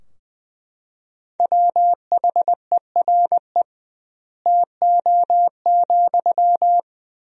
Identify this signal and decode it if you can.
Morse code, heard as a keyed tone, about 20 wpm, 700 Hz. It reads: WHERE TO,
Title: 2x Word-Spacing - Sentences from Top 200 Words - 20wpm